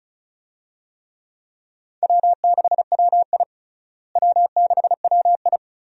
Morse code, heard as a keyed tone, about 36 wpm, 700 Hz. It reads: W6WI W6WI
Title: USA Call Signs - 4 Characters - Repeated 2x - 35wpm